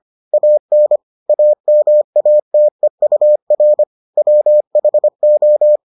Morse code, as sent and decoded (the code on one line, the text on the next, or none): .- -. / .- -- .- - . ..- .-. / .-- .... ---
AN AMATEUR WHO